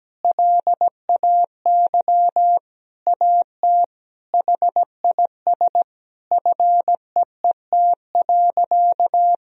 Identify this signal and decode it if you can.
Morse code, heard as a keyed tone, about 17 wpm, 700 Hz. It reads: LAY AT HIS FEET.
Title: Sentences from Top 1000 Words - Review Part 2 - 17wpm